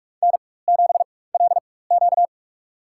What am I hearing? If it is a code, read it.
Morse code, heard as a keyed tone, about 44 wpm, spaced as for 21 wpm, 700 Hz. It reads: N7LQ
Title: USA Call Signs - 4 Characters - 4x Character Spacing - 45wpm